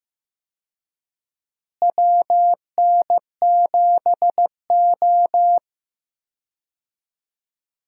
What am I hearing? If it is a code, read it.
Morse code, heard as a keyed tone, about 15 wpm, 700 Hz. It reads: WN7O
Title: USA Call Signs - 4 Characters - Repeated 3x - 15wpm